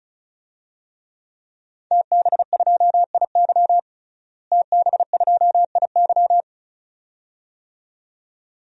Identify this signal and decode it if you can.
Morse code, heard as a keyed tone, about 36 wpm, 700 Hz. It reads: TB2IY TB2IY
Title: DXCC-100 Call Signs - 5 Characters - Repeated 2x - 35wpm